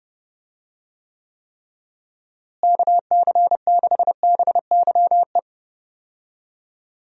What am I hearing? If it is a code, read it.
Morse code, heard as a keyed tone, about 30 wpm, 700 Hz. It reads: KC6BYE
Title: USA Call Signs - 6 Characters - Repeated 3x - 30wpm